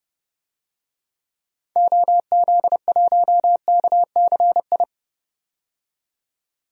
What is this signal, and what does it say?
Morse code, heard as a keyed tone, about 32 wpm, 700 Hz. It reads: OZ1KCI
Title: DXCC-100 Call Signs - 6 Characters - 30wpm